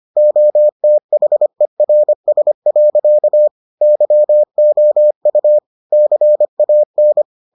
Morse code, heard as a keyed tone, about 25 wpm, 600 Hz. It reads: OTHERS. YOU CAN